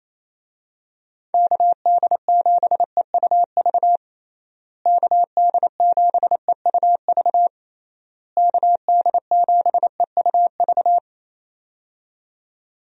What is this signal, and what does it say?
Morse code, heard as a keyed tone, about 27 wpm, 700 Hz. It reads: KD7EUV KD7EUV KD7EUV
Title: USA Call Signs - 6 Characters - Repeated 3x - 28wpm